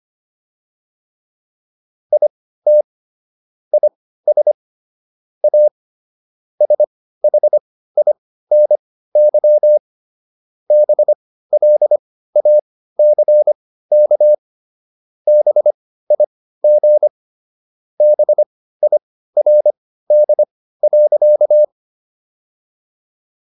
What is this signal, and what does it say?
Morse code, heard as a keyed tone, about 25 wpm, spaced as for 15 wpm, 600 Hz. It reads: IT IS A SHINY BLACK BIG BIRD.